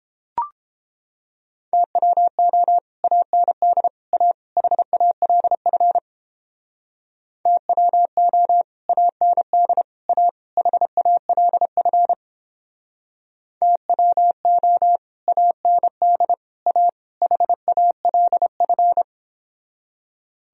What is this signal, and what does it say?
Morse code, heard as a keyed tone, about 31 wpm, 700 Hz. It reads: TWO AND A HALF TWO AND A HALF TWO AND A HALF